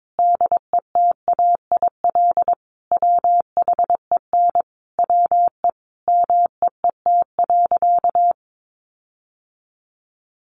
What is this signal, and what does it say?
Morse code, heard as a keyed tone, about 22 wpm, 700 Hz. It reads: DETAIL WHEN WE MEET.
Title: Sentences from Top 1500 Words 22wpm